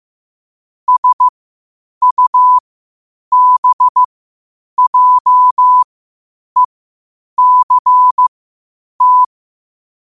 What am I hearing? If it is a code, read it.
Morse code, heard as a keyed tone, about 15 wpm, spaced as for 8 wpm, 1000 Hz. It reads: SUBJECT